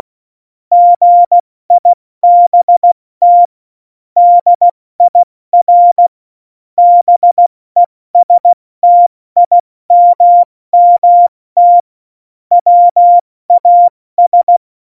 Morse code, heard as a keyed tone, about 16 wpm, spaced as for 14 wpm, 700 Hz. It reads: GIBT DIR BESTIMMT WAS
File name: Morsekurs / Witze / Witz 08. Dezember 2020, 70 F80